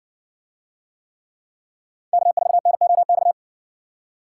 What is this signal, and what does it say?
Morse code, heard as a keyed tone, about 70 wpm, 700 Hz. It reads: K4NOX